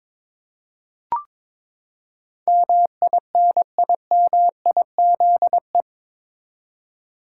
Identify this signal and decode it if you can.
Morse code, heard as a keyed tone, about 22 wpm, 700 Hz. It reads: MINIMIZE
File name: practice / 22wpm / Eight Letter Words - 22wpm